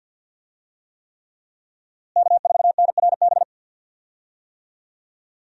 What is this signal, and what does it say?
Morse code, heard as a keyed tone, about 49 wpm, 700 Hz. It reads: K4NRB
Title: USA Call Signs - 5 Characters - Repeated 3x - 50wpm